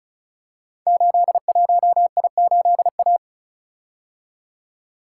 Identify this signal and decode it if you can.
Morse code, heard as a keyed tone, about 36 wpm, 690 Hz. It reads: 81I8A
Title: Code Groups - 5 Letter:Numbers - 35wpm